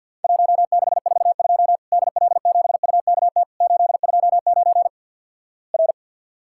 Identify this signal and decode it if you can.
Morse code, heard as a keyed tone, about 54 wpm, 700 Hz. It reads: 1642 DL7UKT 829 R
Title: Worked All Europe DX Contest - QTC Exchange - 50wpm